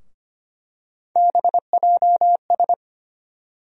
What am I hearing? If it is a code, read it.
Morse code, heard as a keyed tone, about 25 wpm, 700 Hz. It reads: BJS
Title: USA Call Sign Postfixes - 3 Letters 25wpm